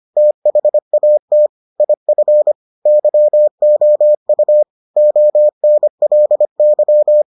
Morse code, heard as a keyed tone, about 25 wpm, 600 Hz. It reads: THAT IF YOU ONLY